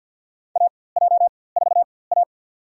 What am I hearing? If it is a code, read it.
Morse code, heard as a keyed tone, about 52 wpm, spaced as for 24 wpm, 700 Hz. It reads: AJ4A
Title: USA Call Signs - 4x Character Spacing - 50wpm